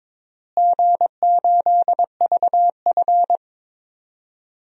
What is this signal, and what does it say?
Morse code, heard as a keyed tone, about 22 wpm, 700 Hz. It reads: G8VF
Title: DXCC-100 Call Signs - 4 Characters - Repeated 2x - 22wpm